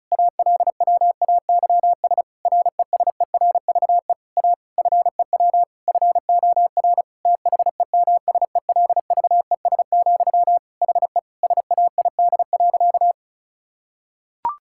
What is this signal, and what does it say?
Morse code, heard as a keyed tone, about 34 wpm, 700 Hz. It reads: ALWAYS RESERVE A FEW FOR THEMSELVES, HE SAID.
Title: Sentences from Top 1000 Words - Review Part 6 - 35wpm